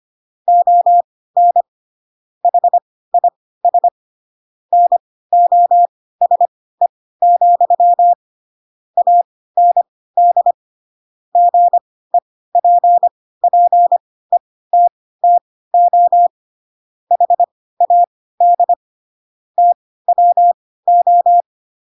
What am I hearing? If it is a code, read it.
Morse code, heard as a keyed tone, about 25 wpm, spaced as for 16 wpm, 700 Hz. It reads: ON HIS NOSE, AND GEPPETTO HAD TWO
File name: Pinocchio / Pinocchio ch2 10 25 ik1hge